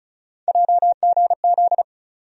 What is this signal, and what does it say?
Morse code, heard as a keyed tone, about 36 wpm, 700 Hz. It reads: JGZ
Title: USA Call Sign Postfixes - 3 Letters 35wpm